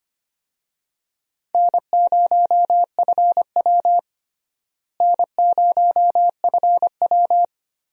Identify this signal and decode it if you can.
Morse code, heard as a keyed tone, about 25 wpm, 700 Hz. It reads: N0FW N0FW
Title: USA Call Signs - 4 Characters - Repeated 2x - 25wpm